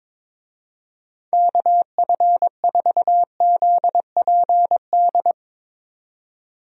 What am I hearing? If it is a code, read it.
Morse code, heard as a keyed tone, about 22 wpm, 700 Hz. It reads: KF4ZPD
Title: USA Call Signs - 6 Characters - 22wpm